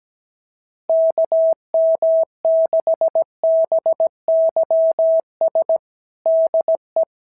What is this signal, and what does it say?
Morse code, heard as a keyed tone, about 17 wpm, 650 Hz. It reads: KM6BYS DE
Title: SOTA QSO Practice 17wpm